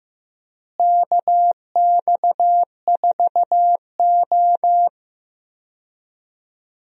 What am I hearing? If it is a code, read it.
Morse code, heard as a keyed tone, about 15 wpm, 700 Hz. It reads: KX4O
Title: USA Call Signs - 4 Characters - Repeated 2x - 15wpm